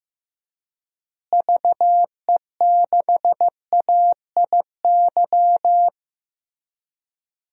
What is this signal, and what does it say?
Morse code, heard as a keyed tone, about 15 wpm, 700 Hz. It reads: VE6AIY
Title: DXCC-100 Call Signs - 6 Characters - Repeated 3x - 15wpm